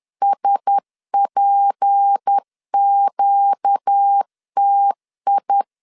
Morse code, heard as a keyed tone, about 10 wpm, 800 Hz. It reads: SPQTI
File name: LEZIONI / 6 PKX0 / 6 P 10wpm 2432